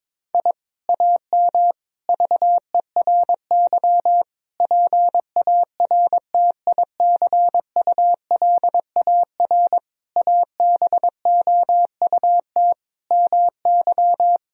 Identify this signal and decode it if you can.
Morse code, heard as a keyed tone, about 22 wpm, 700 Hz. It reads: I AM VERY PARTICULAR ABOUT MY
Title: Sentences from Top 1000 Words - Review Part 4 - 22wpm